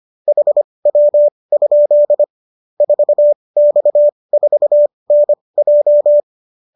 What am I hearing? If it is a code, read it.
Morse code, heard as a keyed tone, about 25 wpm, spaced as for 20 wpm, 590 Hz. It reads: HW? 4X4NJ